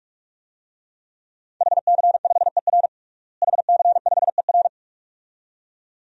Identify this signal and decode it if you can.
Morse code, heard as a keyed tone, about 46 wpm, 700 Hz. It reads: HC5ER HC5ER
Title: DXCC-100 Call Signs - 5 Characters - Repeated 2x - 45wpm